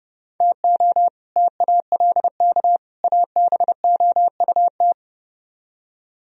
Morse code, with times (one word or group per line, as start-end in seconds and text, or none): 0.4-1.1 TO
1.4-2.8 TALK
3.0-4.9 ABOUT